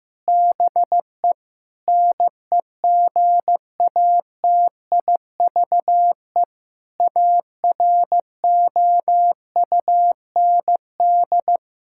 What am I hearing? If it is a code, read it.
Morse code, heard as a keyed tone, about 16 wpm, 700 Hz. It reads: BE NEGATIVE AROUND
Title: Sentences from Top 1000 Words - Review Part 6 - 15wpm